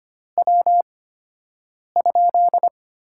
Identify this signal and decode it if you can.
Morse code, heard as a keyed tone, about 25 wpm, spaced as for 7 wpm, 700 Hz. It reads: W?
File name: Top 90 QSO Elements - 8x Character Spacing - 25wpm